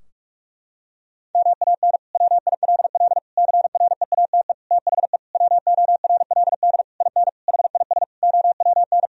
Morse code, heard as a keyed tone, about 46 wpm, 700 Hz. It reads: MAN WILL CREATE THE WORLD IN HIS OWN
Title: Sentences from Top 1000 Words - Review Part 4 - 45wpm